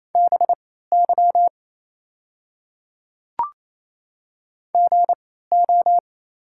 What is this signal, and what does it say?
Morse code, heard as a keyed tone, about 27 wpm, spaced as for 16 wpm, 700 Hz. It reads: BY GO
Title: 3x Character Spacing - Top 200 Words - 28wpm